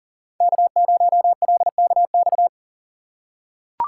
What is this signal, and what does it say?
Morse code, heard as a keyed tone, about 40 wpm, 700 Hz. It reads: K0LKX
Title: USA Call Signs - 5 Characters - 6x Character Spacing - 40wpm